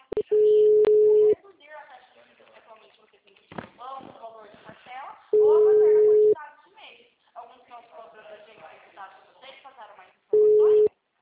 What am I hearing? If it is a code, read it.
Ringback tone (Europe, 425 Hz), with speech.